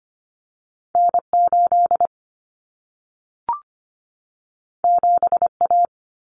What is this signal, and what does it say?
Morse code, heard as a keyed tone, about 25 wpm, 700 Hz. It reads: N8 7A
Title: Common Cut Numbers - 2-Digit 25wpm